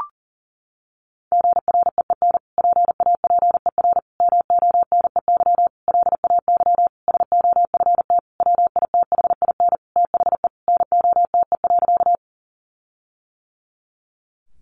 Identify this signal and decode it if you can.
Morse code, heard as a keyed tone, about 40 wpm, 700 Hz. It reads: GREEN PAPER MONEY LAY SOFT WITHIN THE NOTE.
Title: Sentences from Top 1500 Words 40wpm